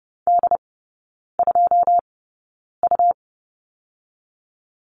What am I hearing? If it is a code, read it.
Morse code, heard as a keyed tone, about 30 wpm, spaced as for 9 wpm, 700 Hz. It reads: D2U
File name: Lesson 17e - Letter u - Three Characters 30wpm